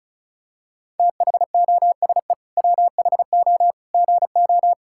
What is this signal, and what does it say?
Morse code, heard as a keyed tone, about 36 wpm, 700 Hz. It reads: THOSE WHO GO
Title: Sentences from Top 1000 Words - Review Part 5 - 35wpm